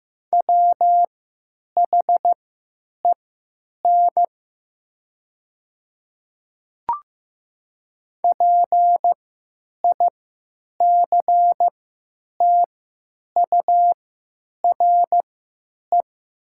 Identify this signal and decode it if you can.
Morse code, heard as a keyed tone, about 15 wpm, spaced as for 8 wpm, 700 Hz. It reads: WHEN PICTURE
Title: 3x Character Spacing - Top 200 Words - 15wpm